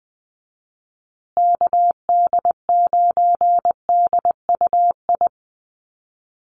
Morse code, heard as a keyed tone, about 20 wpm, 700 Hz. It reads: KD9DUI
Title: USA Call Signs Repeated 3x 20wpm